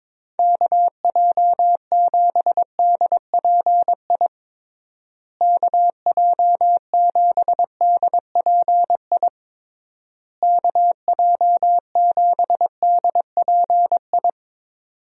Morse code, heard as a keyed tone, about 22 wpm, 700 Hz. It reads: KJ7DPI KJ7DPI KJ7DPI
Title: USA Call Signs - 6 Characters - Repeated 3x - 22wpm